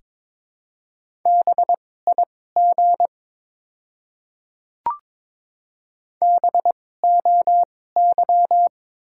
Morse code, heard as a keyed tone, about 22 wpm, spaced as for 16 wpm, 700 Hz. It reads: BIG BOY
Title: 2x Character Spacing - Top 200 Words - 22wpm